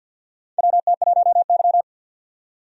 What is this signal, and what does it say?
Morse code, heard as a keyed tone, about 53 wpm, 700 Hz. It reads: WT1Y
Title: USA Call Signs - 4 Characters - 50wpm